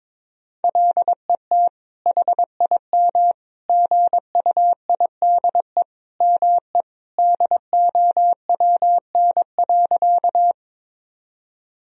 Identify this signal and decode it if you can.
Morse code, heard as a keyed tone, about 22 wpm, 700 Hz. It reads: LET HIM GUIDE ME DOWN.
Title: Sentences from Top 1000 Words - Review Part 3 - 22wpm